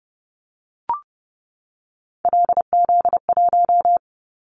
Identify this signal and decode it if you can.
Morse code, heard as a keyed tone, about 30 wpm, 700 Hz. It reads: LZ1
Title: Callsign Prefixes for Top 25 DXCC - 30wpm